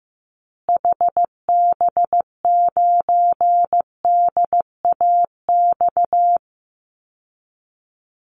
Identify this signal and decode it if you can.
Morse code, heard as a keyed tone, about 15 wpm, 700 Hz. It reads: HB9DAX
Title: Call Signs Repeated 2x - 15wpm